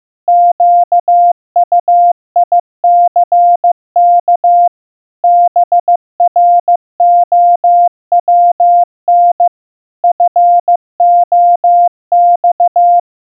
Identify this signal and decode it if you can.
Morse code, heard as a keyed tone, about 15 wpm, 700 Hz. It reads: QUICK BROWN FOX
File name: morse-practice-15wpm